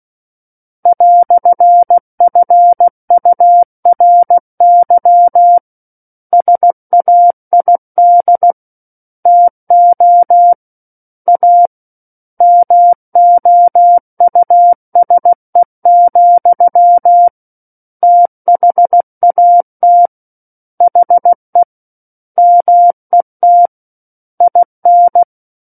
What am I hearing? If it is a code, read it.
Morse code, heard as a keyed tone, about 16 wpm, 700 Hz. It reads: "FURY SAID TO A MOUSE, THAT HE MET IN
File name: 03-Chapter-03 16 WPM